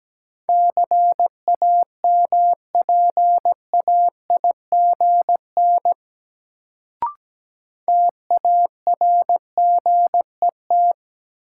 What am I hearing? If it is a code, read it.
Morse code, heard as a keyed tone, about 17 wpm, 700 Hz. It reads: CAMPAIGN TARGET
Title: Sentences from Top 1200 Words - New Word Drill Alt - 17wpm